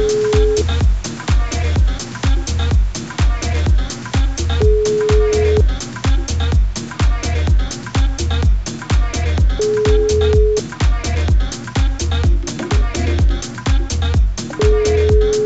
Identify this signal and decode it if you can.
Ringback tone (Europe, 425 Hz), with speech.